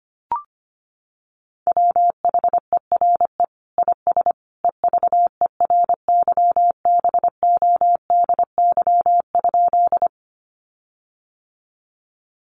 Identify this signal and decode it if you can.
Morse code, heard as a keyed tone, about 25 wpm, 700 Hz. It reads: WHERE IS EVERYBODY?
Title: Sentences from Top 1400 Words 25wpm